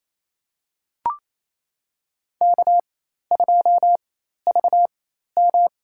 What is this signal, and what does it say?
Morse code, heard as a keyed tone, about 28 wpm, spaced as for 13 wpm, 700 Hz. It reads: K2VM
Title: USA Call Signs - 4 Characters - 4x Character Spacing - 28wpm